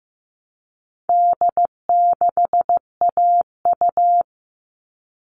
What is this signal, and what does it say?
Morse code, heard as a keyed tone, about 15 wpm, 700 Hz. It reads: D6AU
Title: All Cut Numbers - 4-Digit 15wpm